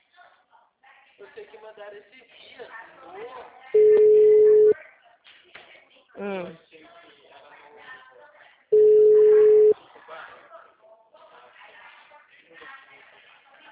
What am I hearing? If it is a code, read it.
Ringback tone (Europe, 425 Hz), with speech.